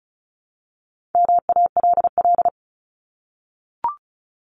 Morse code, heard as a keyed tone, about 34 wpm, 700 Hz. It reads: MALL
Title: Top 1501-1750 Words 35wpm